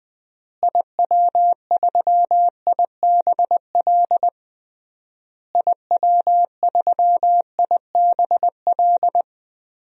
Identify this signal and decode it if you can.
Morse code, heard as a keyed tone, about 21 wpm, 700 Hz. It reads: IW3IBL IW3IBL
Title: DXCC-100 Call Signs - 6 Characters - Repeated 3x - 20wpm